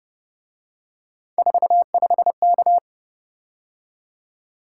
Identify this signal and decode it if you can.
Morse code, heard as a keyed tone, about 29 wpm, 700 Hz. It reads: 45K